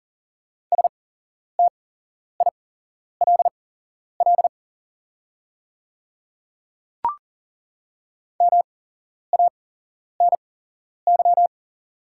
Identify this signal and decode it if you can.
Morse code, heard as a keyed tone, about 40 wpm, spaced as for 11 wpm, 690 Hz. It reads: STILL MANY
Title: 8x Character Spacing - Top 300 Words - 40wpm